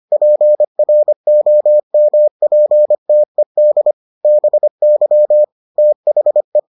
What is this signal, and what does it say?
Morse code, heard as a keyed tone, about 25 wpm, 600 Hz. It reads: PROMPTED BY THE